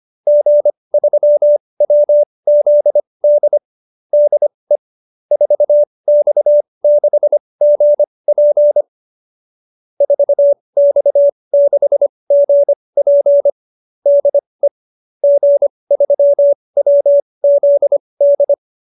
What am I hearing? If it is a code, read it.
Morse code, heard as a keyed tone, about 25 wpm, spaced as for 20 wpm, 590 Hz. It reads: G3WZD DE 4X6GP 4X6GP DE G3WZD
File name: QSO310 20